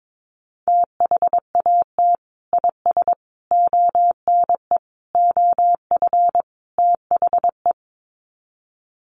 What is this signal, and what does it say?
Morse code, heard as a keyed tone, about 22 wpm, 700 Hz. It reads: THAT IS ONE OF THE
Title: Sets of 5 Words Encore 22wpm